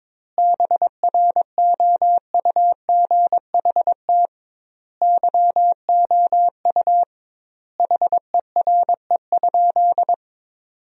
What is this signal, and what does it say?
Morse code, heard as a keyed tone, about 22 wpm, 700 Hz. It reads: BROUGHT YOU HERE?
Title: 2x Word-Spacing - Sentences from Top 500 Words - 22wpm